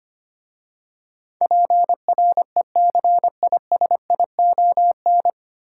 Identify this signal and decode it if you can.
Morse code, heard as a keyed tone, about 25 wpm, 690 Hz. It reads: PRECISION
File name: practice / 25wpm / 9 Letter Words - First 3 Letters Given - 25wpm